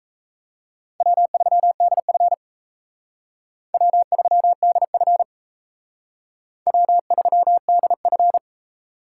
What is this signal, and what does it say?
Morse code, heard as a keyed tone, about 36 wpm, 700 Hz. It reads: W3DF W3DF W3DF